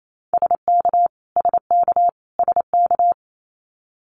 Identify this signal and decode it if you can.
Morse code, heard as a keyed tone, about 28 wpm, 700 Hz. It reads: SK SK SK
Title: Canadian Provinces and Territories - Repeated 3x - 28wpm